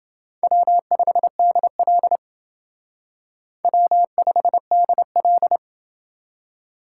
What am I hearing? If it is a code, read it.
Morse code, heard as a keyed tone, about 29 wpm, 700 Hz. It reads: W5DL W5DL